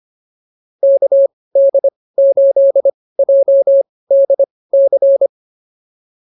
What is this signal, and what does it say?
Morse code, heard as a keyed tone, about 25 wpm, spaced as for 18 wpm, 550 Hz. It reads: KD8JDC